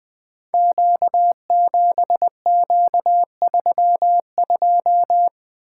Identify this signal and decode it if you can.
Morse code, heard as a keyed tone, about 20 wpm, 700 Hz. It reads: Q7Q32
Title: Code Groups - 5 Letter:Numbers - 20wpm